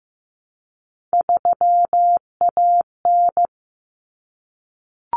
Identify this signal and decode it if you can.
Morse code, heard as a keyed tone, about 15 wpm, 700 Hz. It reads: 3AN
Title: Common Cut Numbers - 3-Digit 15wpm